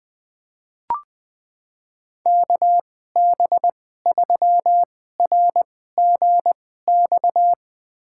Morse code, heard as a keyed tone, about 21 wpm, spaced as for 15 wpm, 700 Hz. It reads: KB3RGX
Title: USA Call Signs - 6 Characters - 2x Character Spacing - 20wpm